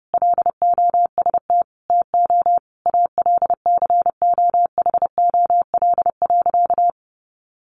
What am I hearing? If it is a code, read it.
Morse code, heard as a keyed tone, about 30 wpm, 700 Hz. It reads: LOST TO ALCOHOL.
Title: Sentences from Top 1400 Words 30wpm